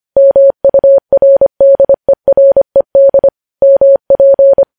Morse code, heard as a keyed tone, about 25 wpm, 550 Hz. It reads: MURDERED MP
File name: podcast / News headlines-1x-25x25